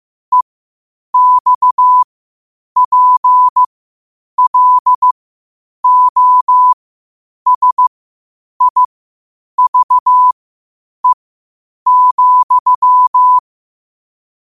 Morse code, heard as a keyed tone, about 15 wpm, spaced as for 8 wpm, 1000 Hz. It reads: EXPLOSIVE,
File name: qotdfiles / 8wpm